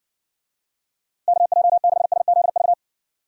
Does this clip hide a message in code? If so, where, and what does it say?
Yes: Morse code from 1.3 to 2.7 s, heard as a keyed tone, about 60 wpm, 700 Hz. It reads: KJ6SZV